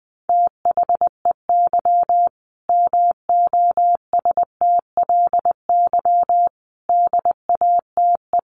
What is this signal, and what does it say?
Morse code, heard as a keyed tone, about 20 wpm, 700 Hz. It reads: THEY MOSTLY DATE
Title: Sentences from Top 1300 Words 20wpm